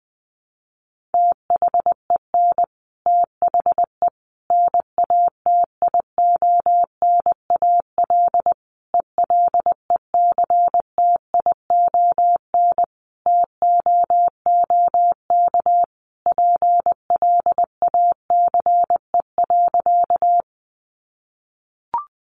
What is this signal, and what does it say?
Morse code, heard as a keyed tone, about 20 wpm, 700 Hz. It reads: THEN THE NATIONAL ELECTION TOOK PLACE.
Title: Sentences from Top 1500 Words 20wpm